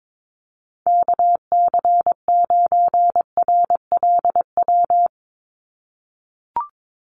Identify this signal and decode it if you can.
Morse code, heard as a keyed tone, about 22 wpm, 700 Hz. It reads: KC9RLW
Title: USA Call Signs 22wpm